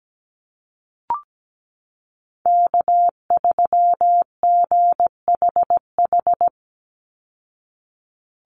Morse code, heard as a keyed tone, about 17 wpm, 700 Hz. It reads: K3GHH
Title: Call Signs Repeated 3x - 17wpm